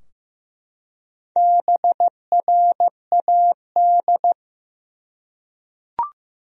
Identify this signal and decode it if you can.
Morse code, heard as a keyed tone, about 15 wpm, 700 Hz. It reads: BRAD
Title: USA Common Names - Max 4 Letters - 15wpm